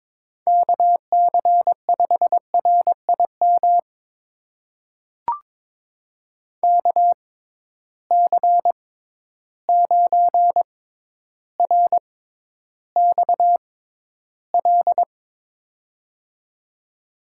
Morse code, heard as a keyed tone, about 22 wpm, spaced as for 11 wpm, 700 Hz. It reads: KC5RIM KC9RXL
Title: USA Call Signs - 6 Characters - 6x Character Spacing - 22wpm